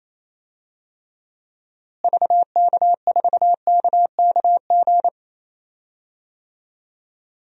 Morse code, heard as a keyed tone, about 28 wpm, 700 Hz. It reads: VK4KKG
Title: DXCC-100 Call Signs - 6 Characters - Repeated 2x - 28wpm